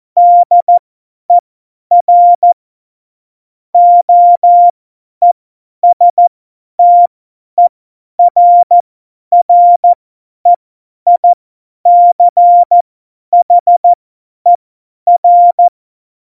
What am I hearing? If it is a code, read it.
Morse code, heard as a keyed tone, about 14 wpm, spaced as for 10 wpm, 700 Hz. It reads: DER OESTERREICHER
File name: Morsekurs / Witze / Witz 24. November 2020, 50 F70